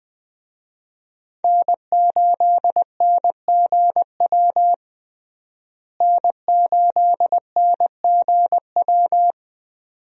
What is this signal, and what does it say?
Morse code, heard as a keyed tone, about 20 wpm, 700 Hz. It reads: N8NGW N8NGW
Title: USA Call Signs - 5 Characters - Repeated 2x - 20wpm